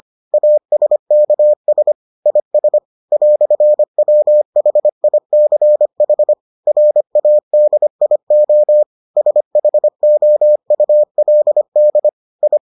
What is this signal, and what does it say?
Morse code, heard as a keyed tone, about 25 wpm, 600 Hz. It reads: ASKS IS "WHICH RADIO SHOULD I